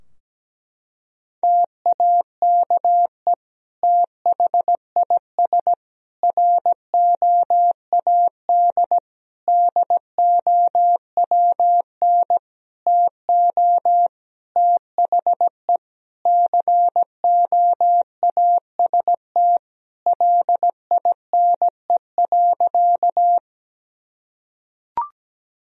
Morse code, heard as a keyed tone, about 17 wpm, 700 Hz. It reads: TAKE THIS ROAD DOWN TO THE COAST LINE.
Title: Sentences from Top 1000 Words - Review Part 4 - 17wpm